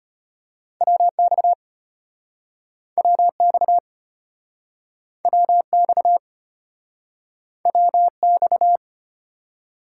Morse code, heard as a keyed tone, about 33 wpm, 700 Hz. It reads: WX WX WX WX